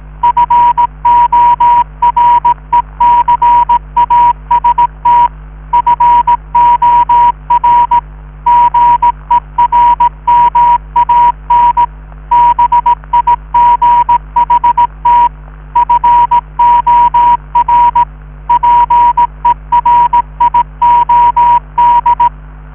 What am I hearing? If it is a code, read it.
Morse code, heard as a keyed tone, about 18 wpm, 960 Hz, with radio noise. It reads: FORECAST FOR GERMAN BIGHT FOR PERIOD